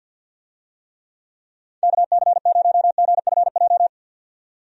Morse code, heard as a keyed tone, about 52 wpm, 700 Hz. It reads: KC0GFJ
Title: USA Call Signs - 6 Characters - 50wpm